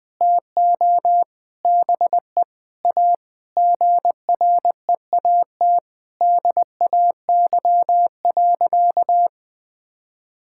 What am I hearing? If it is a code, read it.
Morse code, heard as a keyed tone, about 20 wpm, 700 Hz. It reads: TO BE A GREAT DAY.
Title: Sentences with 6-Word Phrases - Spoken Beginning - 20wpm